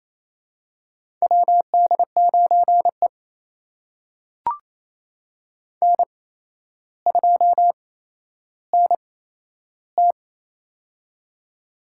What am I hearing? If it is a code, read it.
Morse code, heard as a keyed tone, about 28 wpm, spaced as for 12 wpm, 700 Hz. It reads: WD9E N2NT
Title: USA Call Signs - 4 Characters - 8x Character Spacing - 28wpm